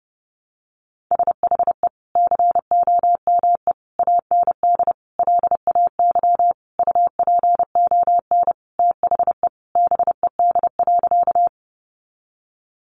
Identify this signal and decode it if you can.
Morse code, heard as a keyed tone, about 30 wpm, 700 Hz. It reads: SHE COME AND LAY UPON THE BED.
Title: Sentences from Top 1100 Words 30wpm